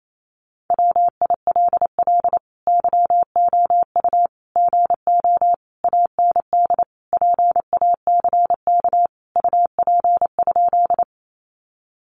Morse code, heard as a keyed tone, about 28 wpm, 700 Hz. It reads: WILL YOU GO AND PACK UP?
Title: Sentences from Top 1300 Words 28wpm